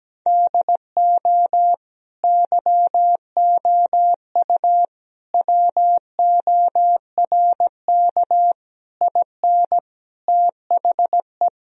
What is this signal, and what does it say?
Morse code, heard as a keyed tone, about 17 wpm, 700 Hz. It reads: DO YOU WORK IN THE
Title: Sentences from Top 1000 Words - Review Part 4 - 17wpm